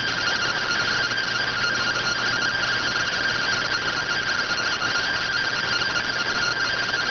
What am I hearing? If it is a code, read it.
Two-tone teleprinter signal, about 170 Hz shift.